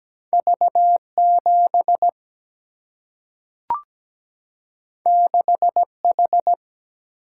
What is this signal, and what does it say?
Morse code, heard as a keyed tone, about 17 wpm, 700 Hz. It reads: V 7 6 H
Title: Single-Letter Single-Number 17wpm